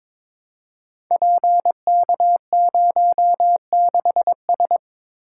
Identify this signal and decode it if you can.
Morse code, heard as a keyed tone, about 22 wpm, 700 Hz. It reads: PK06S
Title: Code Groups - 5 Letter:Numbers - 22wpm